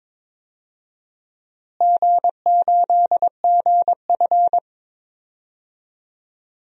Morse code, heard as a keyed tone, about 22 wpm, 700 Hz. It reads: G8GF
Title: DXCC-100 Call Signs - 4 Characters - Repeated 2x - 22wpm